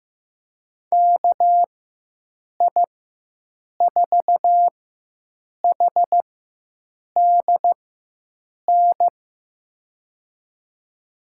Morse code, heard as a keyed tone, about 15 wpm, spaced as for 7 wpm, 700 Hz. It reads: KI4HDN
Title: USA Call Signs - 6 Characters - 4x Character Spacing - 15wpm